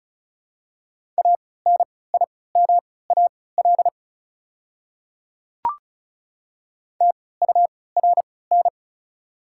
Morse code, heard as a keyed tone, about 36 wpm, spaced as for 20 wpm, 690 Hz. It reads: ANIMAL TURN
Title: 3x Character Spacing - Top 500 Words - 35wpm